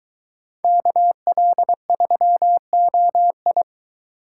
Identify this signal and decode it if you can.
Morse code, heard as a keyed tone, about 22 wpm, 700 Hz. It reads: KL3OI